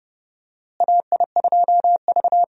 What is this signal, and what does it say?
Morse code, heard as a keyed tone, about 30 wpm, 700 Hz. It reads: AI2V
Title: USA Call Signs - 4 Characters - Repeated 3x - 30wpm